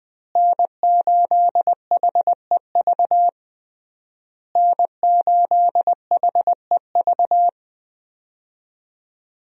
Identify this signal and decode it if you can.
Morse code, heard as a keyed tone, about 21 wpm, 700 Hz. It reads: N8HEV N8HEV
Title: USA Call Signs - 5 Characters - Repeated 3x - 20wpm